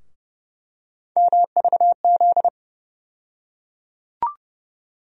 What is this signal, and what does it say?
Morse code, heard as a keyed tone, about 30 wpm, 700 Hz. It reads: MVZ
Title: USA Call Sign Postfixes - 3 Letters 30wpm